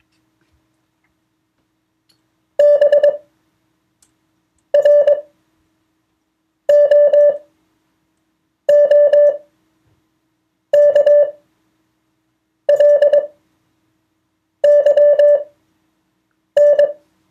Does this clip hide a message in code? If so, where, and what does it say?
Yes: Morse code from 2.6 to 16.9 s, heard as a keyed tone, about 21 wpm, spaced as for 6 wpm, 580 Hz. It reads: BROOKLYN